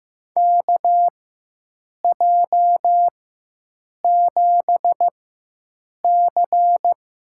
Morse code, heard as a keyed tone, about 15 wpm, spaced as for 7 wpm, 700 Hz. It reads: KJ7C
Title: USA Call Signs - 6 Characters - 4x Character Spacing - 15wpm